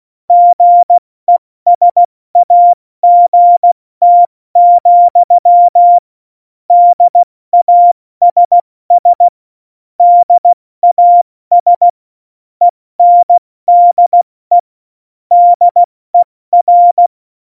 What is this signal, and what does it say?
Morse code, heard as a keyed tone, about 16 wpm, 700 Hz. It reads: GESAGT, DASS DAS ENDE DER
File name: Morsekurs